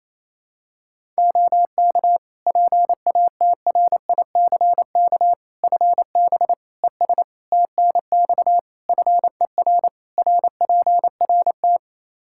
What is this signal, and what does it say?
Morse code, heard as a keyed tone, about 28 wpm, 700 Hz. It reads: OK PATRICK FB ES TNX FER RPRT
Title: Practice QSO - Exchange 2 - Info - 28wpm